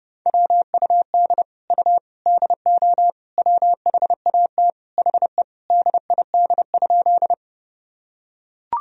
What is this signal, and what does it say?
Morse code, heard as a keyed tone, about 30 wpm, 700 Hz. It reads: WUD U DO WHAT HE DID?
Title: Sentences from Top 100 Words and Abbreviations 30wpm